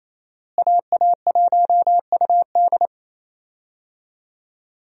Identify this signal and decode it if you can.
Morse code, heard as a keyed tone, about 28 wpm, 700 Hz. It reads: AA1UD
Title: USA Call Signs - 5 Characters - Repeated 2x - 28wpm